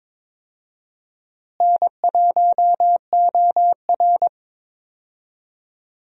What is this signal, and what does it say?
Morse code, heard as a keyed tone, about 22 wpm, 700 Hz. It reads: N1OR
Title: USA Call Signs - 4 Characters - Repeated 2x - 22wpm